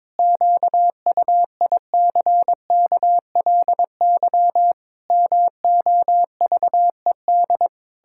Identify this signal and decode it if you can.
Morse code, heard as a keyed tone, about 22 wpm, 700 Hz. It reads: QUICKLY MOVED
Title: Pangram Challenge v2 - 22wpm